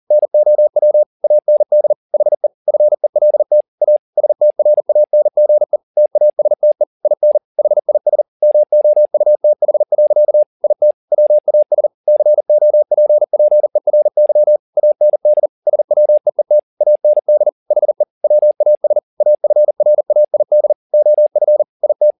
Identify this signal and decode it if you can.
Morse code, heard as a keyed tone, about 40 wpm, 600 Hz. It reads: NOW AND HE FELT A STRANGE TASTE IN HIS MOUTH. IT WAS COPPERY AND SWEET AND HE WAS AFRAID OF IT
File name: the old man and the sea 2 40wpm